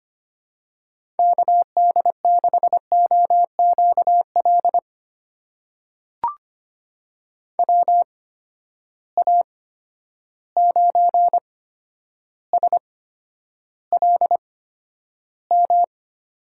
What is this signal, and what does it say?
Morse code, heard as a keyed tone, about 25 wpm, spaced as for 11 wpm, 700 Hz. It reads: KD6OQL WA9SLM
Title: USA Call Signs - 6 Characters - 8x Character Spacing - 25wpm